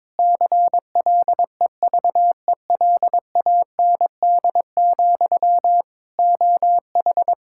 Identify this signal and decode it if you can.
Morse code, heard as a keyed tone, about 22 wpm, 700 Hz. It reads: CLEVELAND, OH
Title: Practice QSO - Exchange 1 - Beginning - 22wpm